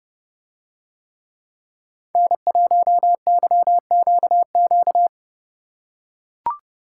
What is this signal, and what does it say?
Morse code, heard as a keyed tone, about 29 wpm, 700 Hz. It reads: N1YQQ